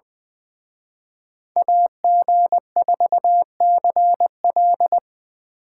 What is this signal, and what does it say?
Morse code, heard as a keyed tone, about 24 wpm, 700 Hz. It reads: AG4CL
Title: Speed Racing - USA 2x2 Call Signs 20wpm